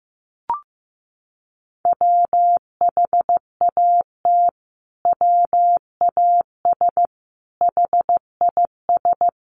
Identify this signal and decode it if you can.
Morse code, heard as a keyed tone, about 15 wpm, 700 Hz. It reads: WHAT WAS HIS
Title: Sentences from Top 1300 Words 15wpm